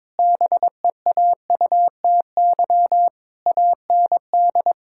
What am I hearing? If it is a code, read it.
Morse code, heard as a keyed tone, about 22 wpm, 700 Hz. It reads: BEAUTY AND
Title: Sentences from Top 1600 Words 22wpm